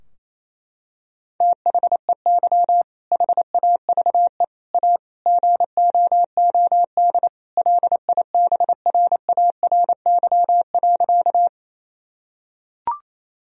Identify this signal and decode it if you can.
Morse code, heard as a keyed tone, about 27 wpm, 700 Hz. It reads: THEY HAVE A GOOD LIBRARY.
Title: Sentences from Top 1000 Words - Review Part 4 - 28wpm